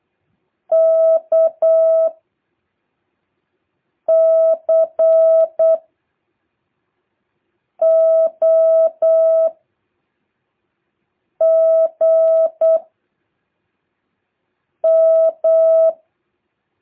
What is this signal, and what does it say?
Morse code, heard as a keyed tone, about 8 wpm, 650 Hz. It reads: K C O G M
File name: jh8cbh / 2023cw